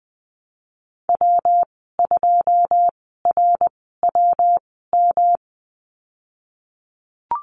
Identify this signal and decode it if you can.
Morse code, heard as a keyed tone, about 20 wpm, spaced as for 15 wpm, 700 Hz. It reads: W2RWM
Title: 128 Lesson 21g - Number 6 - Call Signs 20wpm